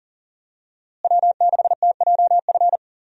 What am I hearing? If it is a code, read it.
Morse code, heard as a keyed tone, about 40 wpm, 700 Hz. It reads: W6TJF
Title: USA Call Signs - 5 Characters - Repeated 3x - 40wpm